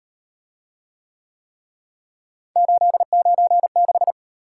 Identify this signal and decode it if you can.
Morse code, heard as a keyed tone, about 35 wpm, 700 Hz. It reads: 896